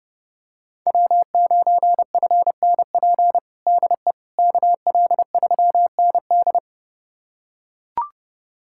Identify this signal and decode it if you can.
Morse code, heard as a keyed tone, about 32 wpm, 700 Hz. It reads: W9FNP DE KL3ND
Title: USA Callsign DE Callsign - 5 Characters - 30wpm